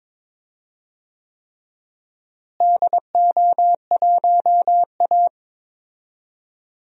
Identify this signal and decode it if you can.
Morse code, heard as a keyed tone, about 21 wpm, 700 Hz. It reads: DO1A
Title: DXCC-100 Call Signs - 4 Characters - Repeated 3x - 22wpm